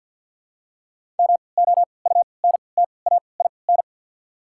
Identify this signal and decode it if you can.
Morse code, heard as a keyed tone, about 53 wpm, spaced as for 29 wpm, 700 Hz. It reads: MOUNTAIN
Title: 3x Character Spacing - Top 300 Words - 50wpm